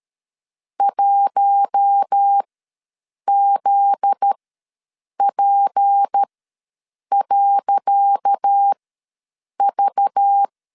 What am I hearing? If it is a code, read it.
Morse code, heard as a keyed tone, about 13 wpm, spaced as for 7 wpm, 800 Hz. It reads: 1ZP.V
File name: LEZIONI / 10 SIGNS / 10C DOT ws 12wpm 3548